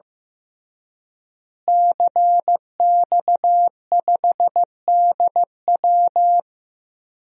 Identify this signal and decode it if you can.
Morse code, heard as a keyed tone, about 15 wpm, 700 Hz. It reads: CX5DW
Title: DXCC-100 Call Signs - 5 Characters - Repeated 3x - 15wpm